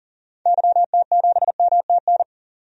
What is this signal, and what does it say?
Morse code, heard as a keyed tone, about 40 wpm, 700 Hz. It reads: YT7MTN
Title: DXCC-100 Call Signs - 6 Characters - 40wpm